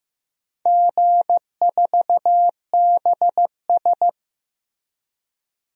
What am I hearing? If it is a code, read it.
Morse code, heard as a keyed tone, about 15 wpm, 700 Hz. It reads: G4BS
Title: DXCC-100 Call Signs - 4 Characters - Repeated 2x - 15wpm